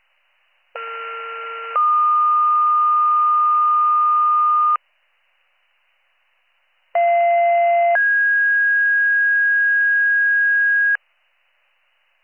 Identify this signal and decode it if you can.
Two-tone page, heard as a one-second tone, then a three-second tone, about 1370 then 1220 Hz, about 690 then 1650 Hz. It is heard from 0.8 to 11.0 s.